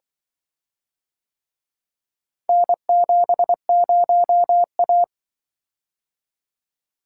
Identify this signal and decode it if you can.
Morse code, heard as a keyed tone, about 24 wpm, 700 Hz. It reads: N70A
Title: Speed-Racing - Four Letter-Number Combinations - 20wpm